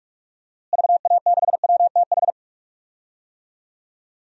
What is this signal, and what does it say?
Morse code, heard as a keyed tone, about 41 wpm, 700 Hz. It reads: VA6WTH
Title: DXCC-100 Call Signs - 6 Characters - Repeated 2x - 45wpm